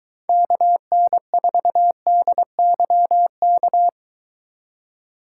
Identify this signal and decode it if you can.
Morse code, heard as a keyed tone, about 23 wpm, 700 Hz. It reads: KN4DYK